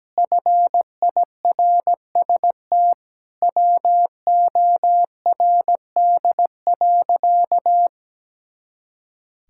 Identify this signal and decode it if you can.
Morse code, heard as a keyed tone, about 17 wpm, 700 Hz. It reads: FIRST WORD.
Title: Pangram Challenge v2 - 17wpm